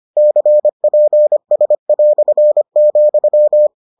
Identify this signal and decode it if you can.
Morse code, heard as a keyed tone, about 25 wpm, 600 Hz. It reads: CPS",